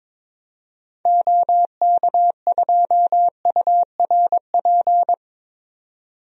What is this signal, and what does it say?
Morse code, heard as a keyed tone, about 22 wpm, 700 Hz. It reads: OK2URP
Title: DXCC-100 Call Signs - 6 Characters - 22wpm